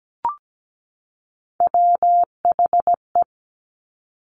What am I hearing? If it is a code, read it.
Morse code, heard as a keyed tone, about 17 wpm, 700 Hz. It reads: WHE
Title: Top Three Letter Combinations in Words 17wpm